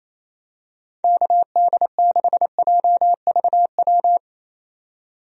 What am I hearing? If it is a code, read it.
Morse code, heard as a keyed tone, about 28 wpm, 700 Hz. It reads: KD6JVW
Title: USA Call Signs - 6 Characters - 28wpm